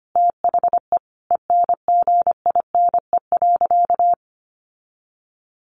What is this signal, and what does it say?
Morse code, heard as a keyed tone, about 25 wpm, 700 Hz. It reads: THE ENGINE.
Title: Sentences from Top 1500 Words 25wpm